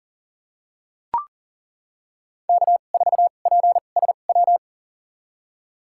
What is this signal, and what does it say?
Morse code, heard as a keyed tone, about 40 wpm, spaced as for 29 wpm, 700 Hz. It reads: K4PSW
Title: USA Call Signs - 5 Characters - 2x Character Spacing - 40wpm